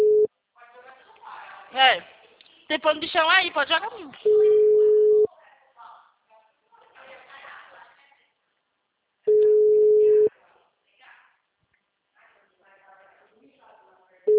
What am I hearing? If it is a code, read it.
Ringback tone (Europe, 425 Hz), with speech.
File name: out- -2002- - - .86292